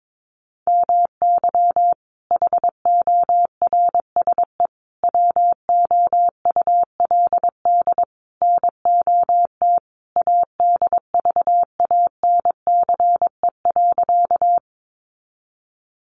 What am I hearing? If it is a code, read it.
Morse code, heard as a keyed tone, about 22 wpm, 700 Hz. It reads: MY HORSE WOULD NOT ADVANCE.
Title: Sentences from Top 1400 Words 22wpm